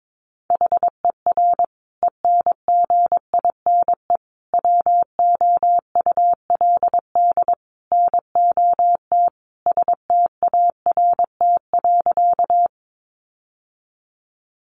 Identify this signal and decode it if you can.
Morse code, heard as a keyed tone, about 22 wpm, 700 Hz. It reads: HER ENGINE WOULD NOT START.
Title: Sentences from Top 1500 Words 22wpm